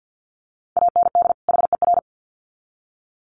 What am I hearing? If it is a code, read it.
Morse code, heard as a keyed tone, about 50 wpm, 700 Hz. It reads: AND HER
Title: Sets of 2 Words 50wpm